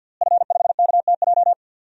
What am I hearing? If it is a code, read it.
Morse code, heard as a keyed tone, about 46 wpm, 700 Hz. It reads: F5KTJ
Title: DXCC-100 Call Signs - 5 Characters - Repeated 3x - 50wpm